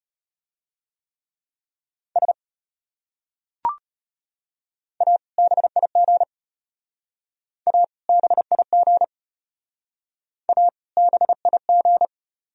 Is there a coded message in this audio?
Yes: Morse code from 2.2 to 12.1 s, heard as a keyed tone, about 32 wpm, 700 Hz. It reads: S A BIG A BIG A BIG